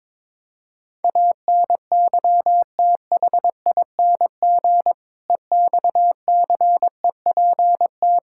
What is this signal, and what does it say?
Morse code, heard as a keyed tone, about 22 wpm, 700 Hz. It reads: ANYTHING EXCEPT
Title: Sentences from Top 1600 Words 22wpm